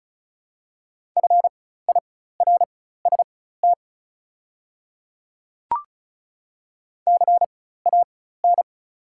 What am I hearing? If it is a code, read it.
Morse code, heard as a keyed tone, about 34 wpm, spaced as for 16 wpm, 700 Hz. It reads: FIRST CAN
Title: 4x Character Spacing - Top 100 Words - 35wpm